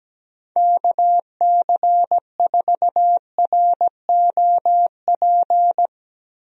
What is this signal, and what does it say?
Morse code, heard as a keyed tone, about 17 wpm, 700 Hz. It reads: KC4ROP
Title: USA Call Signs - 6 Characters - 17wpm